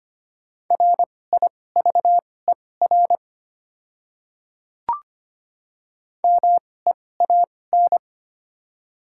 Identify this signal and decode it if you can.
Morse code, heard as a keyed tone, about 25 wpm, spaced as for 18 wpm, 700 Hz. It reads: RIVER MEAN
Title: 2x Character Spacing - Top 500 Words - 25wpm